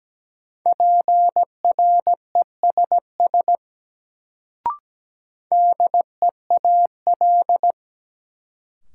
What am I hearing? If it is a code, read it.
Morse code, heard as a keyed tone, about 17 wpm, 700 Hz. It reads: PRESS DEAL
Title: Sentences from Top 800 Words - New Word Drill - 17wpm